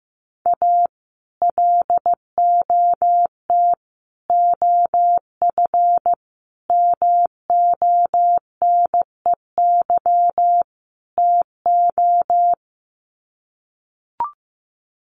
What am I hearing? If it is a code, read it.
Morse code, heard as a keyed tone, about 15 wpm, 700 Hz. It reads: A LOT OF MONEY TO
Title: Sets of 5 Words Encore2 15wpm